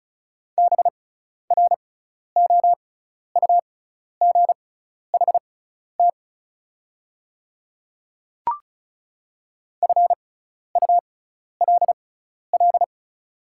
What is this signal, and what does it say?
Morse code, heard as a keyed tone, about 36 wpm, spaced as for 12 wpm, 690 Hz. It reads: BROUGHT FULL
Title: 6x Character Spacing - Top 500 Words - 35wpm